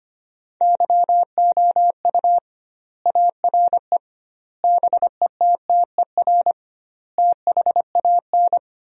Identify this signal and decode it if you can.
Morse code, heard as a keyed tone, about 25 wpm, 700 Hz. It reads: YOU ARE BETTER THAN
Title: 2x Word-Spacing - Sentences from Top 400 Words - 25wpm